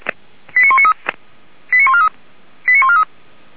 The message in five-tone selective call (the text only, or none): 89193 89143 89143